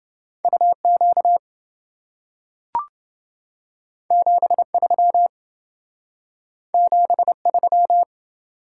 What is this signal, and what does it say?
Morse code, heard as a keyed tone, about 27 wpm, 700 Hz. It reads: UQ 73 73